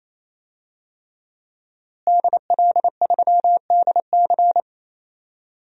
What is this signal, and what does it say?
Morse code, heard as a keyed tone, about 28 wpm, 700 Hz. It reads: DL3DC
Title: DXCC-100 Call Signs - 5 Characters - Repeated 3x - 28wpm